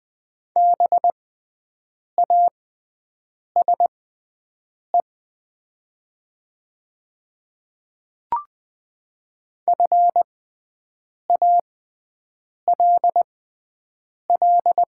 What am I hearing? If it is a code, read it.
Morse code, heard as a keyed tone, about 20 wpm, spaced as for 7 wpm, 700 Hz. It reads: BASE FALL
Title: 6x Character Spacing - Top 500 Words - 20wpm